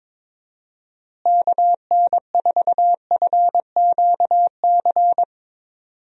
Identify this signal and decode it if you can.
Morse code, heard as a keyed tone, about 21 wpm, 700 Hz. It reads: KN4FQC
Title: USA Call Signs - 6 Characters - Repeated 2x - 22wpm